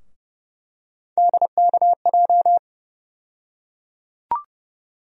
Morse code, heard as a keyed tone, about 31 wpm, 700 Hz. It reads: DKJ
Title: USA Call Sign Postfixes - 3 Letters 30wpm